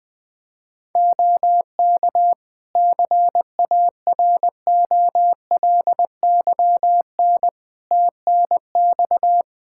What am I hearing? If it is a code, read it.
Morse code, heard as a keyed tone, about 20 wpm, 700 Hz. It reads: OK CAROLYN TNX
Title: Practice QSO - Exchange 4 - Ending - 20wpm